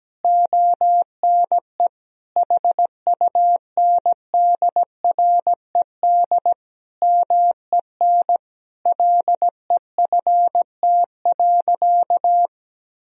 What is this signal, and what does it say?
Morse code, heard as a keyed tone, about 17 wpm, 700 Hz. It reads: ONE HUNDRED MEN LEFT.
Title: Sentences from Top 1000 Words - Review Part 2 - 17wpm